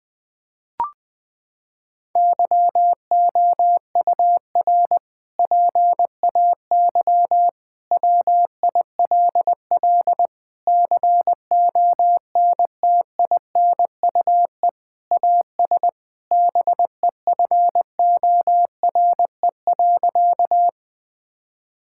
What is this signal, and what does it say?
Morse code, heard as a keyed tone, about 21 wpm, 700 Hz. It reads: YOUR PAY WILL CONTINUE AS BEFORE.
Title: Sentences from Top 1000 Words - Review Part 6 - 20wpm